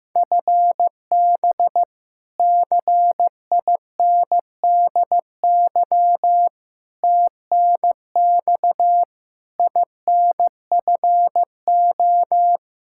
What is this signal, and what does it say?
Morse code, heard as a keyed tone, about 16 wpm, 700 Hz. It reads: FB CINDY TNX INFO
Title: Practice QSO - Exchange 3 - More Info - 15wpm